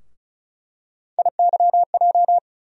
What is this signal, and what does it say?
Morse code, heard as a keyed tone, about 34 wpm, 700 Hz. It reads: IYJ
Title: USA Call Sign Postfixes - 3 Letters 35wpm